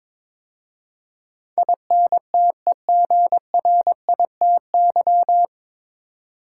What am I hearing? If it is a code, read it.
Morse code, heard as a keyed tone, about 22 wpm, 700 Hz. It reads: INTEGRITY
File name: 9 Letter Words - First 3 Letters Given - 22wpm